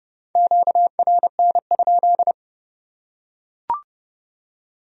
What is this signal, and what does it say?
Morse code, heard as a keyed tone, about 26 wpm, 700 Hz. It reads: QRN?